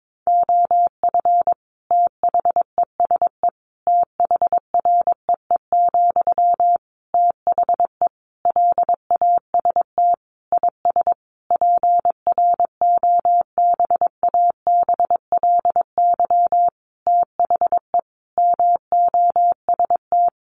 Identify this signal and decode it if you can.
Morse code, heard as a keyed tone, about 22 wpm, 700 Hz. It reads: OF THESE THREE, THE LAST IS PROBABLY THE MOST
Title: Sentences from Top 700 Words 22wpm